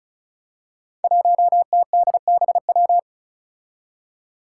Morse code, heard as a keyed tone, about 36 wpm, 700 Hz. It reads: 1TDBW
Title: Five Challenging Letter-Number Combinations 35wpm